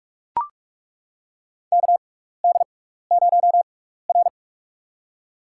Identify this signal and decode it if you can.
Morse code, heard as a keyed tone, about 42 wpm, spaced as for 15 wpm, 700 Hz. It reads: KD0R
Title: USA Call Signs - 6x Character Spacing - 45wpm